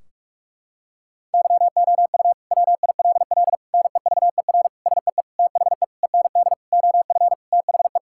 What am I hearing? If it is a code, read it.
Morse code, heard as a keyed tone, about 46 wpm, 700 Hz. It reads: YOU WILL NEVER SEE THE END OF THE
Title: Sentences from Top 1600 Words 45wpm